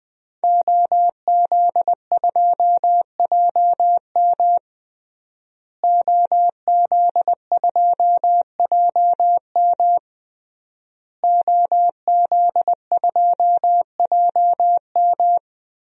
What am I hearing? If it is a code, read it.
Morse code, heard as a keyed tone, about 21 wpm, 700 Hz. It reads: OZ2JM OZ2JM OZ2JM
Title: DXCC-100 Call Signs - 5 Characters - Repeated 3x - 20wpm